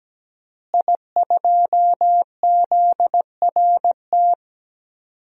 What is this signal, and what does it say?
Morse code, heard as a keyed tone, about 17 wpm, 700 Hz. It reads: I2ZRT
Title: DXCC-100 Call Signs - 5 Characters - Repeated 3x - 17wpm